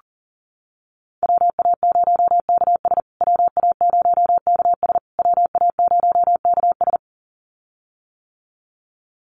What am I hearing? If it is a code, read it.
Morse code, heard as a keyed tone, about 40 wpm, 700 Hz. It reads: WA0KS WA0KS WA0KS
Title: USA Call Signs Repeated 3x 40wpm